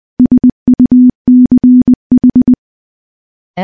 Morse code, heard as a keyed tone, about 20 wpm, 250 Hz. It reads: SUCH
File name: morse-training-20WPM